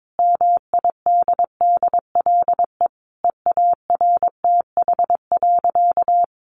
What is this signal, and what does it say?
Morse code, heard as a keyed tone, about 22 wpm, 700 Hz. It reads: MIDDLE EARTH.
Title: Sentences from Top 900 Words 22wpm